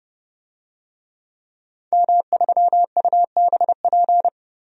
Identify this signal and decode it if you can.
Morse code, heard as a keyed tone, about 30 wpm, 700 Hz. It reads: M3UBP
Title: DXCC-100 Call Signs - 5 Characters - 30wpm